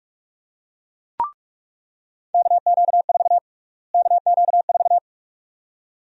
Morse code, heard as a keyed tone, about 43 wpm, 700 Hz. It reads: KQ4 KQ4
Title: USA Call Sign Prefixes - Repeated 2x - 45wpm